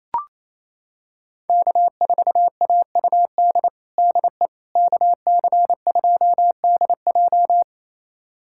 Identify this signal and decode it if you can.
Morse code, heard as a keyed tone, about 28 wpm, 700 Hz. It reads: K4AUD DE KC2DJ
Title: USA Callsign DE Callsign - 5 Characters - 28wpm